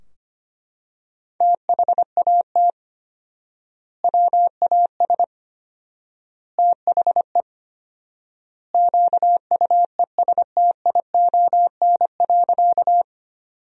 Morse code, heard as a keyed tone, about 25 wpm, 700 Hz. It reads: THAT WAS THE QUESTION.
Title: 4x Word-Spacing - Sentences from Top 400 Words - 25wpm